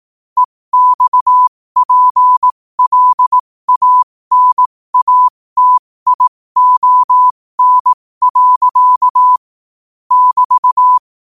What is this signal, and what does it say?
Morse code, heard as a keyed tone, about 18 wpm, spaced as for 15 wpm, 1000 Hz. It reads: EXPLANATION. <BT>
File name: qotdfiles / 15wpm